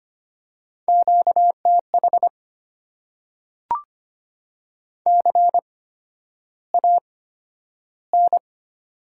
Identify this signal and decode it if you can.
Morse code, heard as a keyed tone, about 25 wpm, spaced as for 11 wpm, 700 Hz. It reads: QTH CAN
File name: Top 90 QSO Elements - 8x Character Spacing - 25wpm